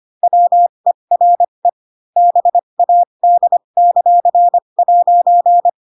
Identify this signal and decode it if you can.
Morse code, heard as a keyed tone, about 25 wpm, spaced as for 22 wpm, 700 Hz. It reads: WERE BAD;'
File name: Pinocchio / Pinocchio ch7 18 25 ik1hge